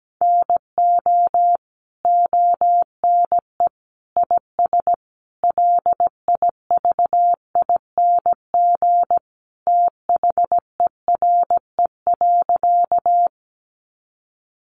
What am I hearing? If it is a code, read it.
Morse code, heard as a keyed tone, about 17 wpm, 700 Hz. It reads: NO ONE IS LIVING THERE.
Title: Sentences from Top 1100 Words 17wpm